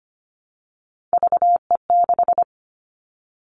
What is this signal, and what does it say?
Morse code, heard as a keyed tone, about 25 wpm, 700 Hz. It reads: VE6
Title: Callsign Prefixes for Top 100 DXCC - 25wpm